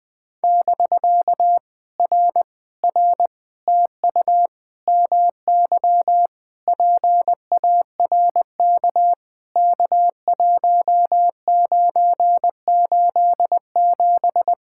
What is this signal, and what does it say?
Morse code, heard as a keyed tone, about 20 wpm, 700 Hz. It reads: <BK> R R TU MY PARK K1987
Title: POTA QSO Practice - 20wpm